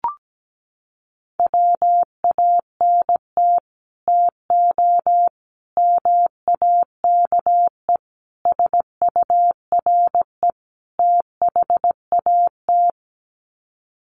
Morse code, heard as a keyed tone, about 17 wpm, 700 Hz. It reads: WANT TO MAKE SURE THAT
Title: Sets of 5 Words 17wpm